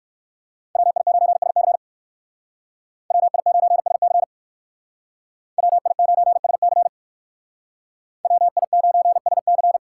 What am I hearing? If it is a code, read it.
Morse code, heard as a keyed tone, about 53 wpm, 700 Hz. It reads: WI9SC WI9SC WI9SC WI9SC
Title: Speed Racing - USA Call Signs 45wpm